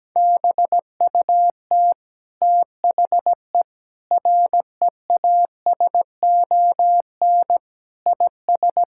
Morse code, heard as a keyed tone, about 17 wpm, 700 Hz. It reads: BUT THE REASON IS
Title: Sentences from Top 1600 Words 17wpm